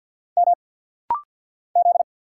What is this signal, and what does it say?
Morse code, heard as a keyed tone, about 50 wpm, 700 Hz. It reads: M Z
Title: Single Letter-Number Alt - Rapid-Fire 50wpm